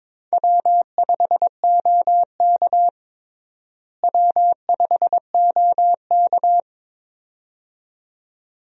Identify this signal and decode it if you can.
Morse code, heard as a keyed tone, about 22 wpm, 700 Hz. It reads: W5OK W5OK
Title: USA Call Signs - 4 Characters - Repeated 2x - 22wpm